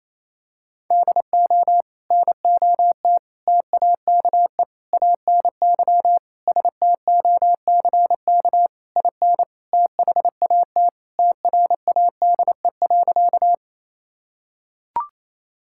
Morse code, heard as a keyed tone, about 28 wpm, 700 Hz. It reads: DO NOT TAKE ANY STOCK IN THAT TRADE.
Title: Sentences from Top 1000 Words - Review Part 5 - 28wpm